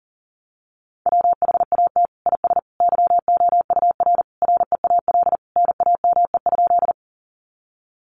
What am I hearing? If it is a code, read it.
Morse code, heard as a keyed tone, about 40 wpm, 700 Hz. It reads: WHAT IS YOUR REAL NAME?
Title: Sentences from Top 300 Words 40wpm